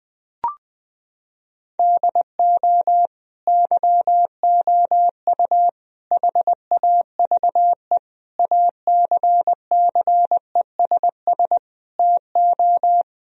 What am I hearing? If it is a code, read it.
Morse code, heard as a keyed tone, about 20 wpm, 700 Hz. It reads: DO YOU HAVE ACCESS TO
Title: Sentences from Top 1000 Words - Review Part 4 - 20wpm